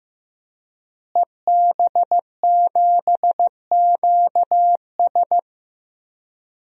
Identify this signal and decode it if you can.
Morse code, heard as a keyed tone, about 15 wpm, 700 Hz. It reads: EB7QS
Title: DXCC-100 Call Signs - 5 Characters - Repeated 3x - 15wpm